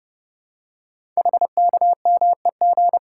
Morse code, heard as a keyed tone, about 30 wpm, 700 Hz. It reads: HKMEG
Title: Code Groups - 5 Letters - 30wpm